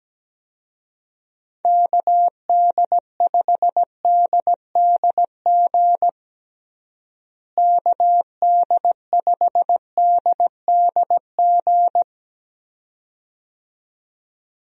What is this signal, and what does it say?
Morse code, heard as a keyed tone, about 17 wpm, 700 Hz. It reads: KD5DDG KD5DDG
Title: USA Call Signs - 6 Characters - Repeated 2x - 17wpm